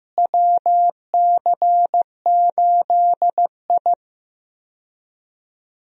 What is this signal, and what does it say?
Morse code, heard as a keyed tone, about 15 wpm, 700 Hz. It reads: WC8I
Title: USA Call Signs - 4 Characters - Repeated 2x - 15wpm